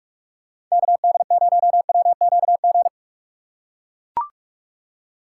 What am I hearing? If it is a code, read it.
Morse code, heard as a keyed tone, about 41 wpm, 700 Hz. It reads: KD0WQG
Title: USA Call Signs - 6 Characters - 3x Character Spacing - 45wpm